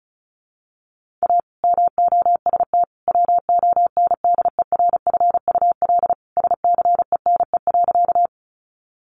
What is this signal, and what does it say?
Morse code, heard as a keyed tone, about 34 wpm, 700 Hz. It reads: A MOST WONDERFUL SCENE.
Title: Sentences from Top 1100 Words 35wpm